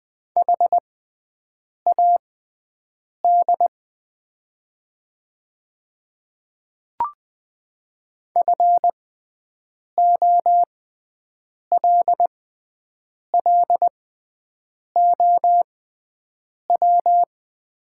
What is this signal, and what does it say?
Morse code, heard as a keyed tone, about 20 wpm, spaced as for 7 wpm, 700 Hz. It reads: HAD FOLLOW
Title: 6x Character Spacing - Top 200 Words - 20wpm